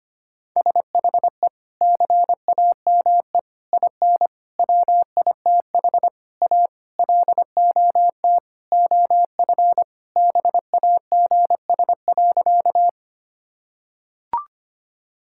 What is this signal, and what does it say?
Morse code, heard as a keyed tone, about 25 wpm, 700 Hz. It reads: SHE CAME IN WITH A LOT OF BAGS.
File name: practice / 25wpm / Sentences with 4-Word Phrases - 25wpm